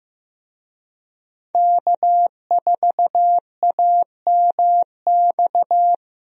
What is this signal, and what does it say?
Morse code, heard as a keyed tone, about 15 wpm, 700 Hz. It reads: K4AMX
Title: USA Call Signs - 5 Characters - Repeated 2x - 15wpm